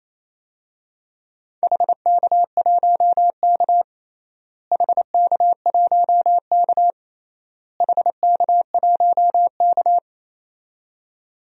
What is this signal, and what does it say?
Morse code, heard as a keyed tone, about 28 wpm, 700 Hz. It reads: HK1K HK1K HK1K
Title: DXCC-100 Call Signs - 4 Characters - Repeated 3x - 28wpm